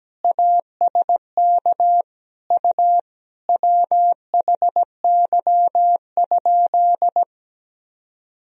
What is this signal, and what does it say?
Morse code, heard as a keyed tone, about 17 wpm, 700 Hz. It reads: ASK U WHY?
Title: Sentences from Top 200 Words and Abbreviations 17wpm